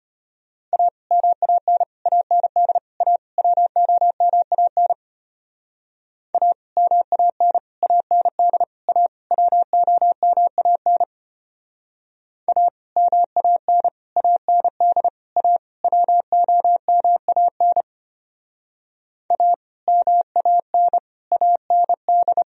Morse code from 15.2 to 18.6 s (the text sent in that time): A WOMAN